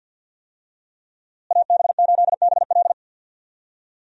Morse code, heard as a keyed tone, about 45 wpm, 700 Hz. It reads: AB8BL
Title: USA Call Signs - 5 Characters - Repeated 3x - 50wpm